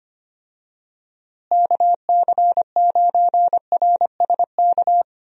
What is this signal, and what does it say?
Morse code, heard as a keyed tone, about 25 wpm, 700 Hz. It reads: KC9RSK
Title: USA Call Signs - 6 Characters - Repeated 2x - 25wpm